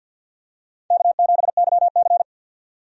Morse code, heard as a keyed tone, about 47 wpm, 700 Hz. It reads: K7YC
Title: USA Call Signs - 4 Characters - 50wpm